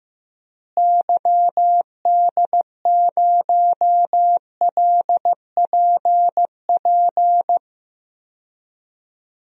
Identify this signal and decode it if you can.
Morse code, heard as a keyed tone, about 16 wpm, 700 Hz. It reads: YD0LPP
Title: DXCC-100 Call Signs - 6 Characters - Repeated 2x - 15wpm